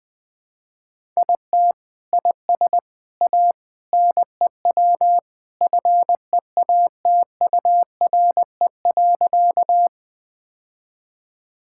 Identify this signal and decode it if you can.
Morse code, heard as a keyed tone, about 21 wpm, 700 Hz. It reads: IT IS A NEW FEATURE.
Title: Sentences from Top 1000 Words - Review Part 5 - 20wpm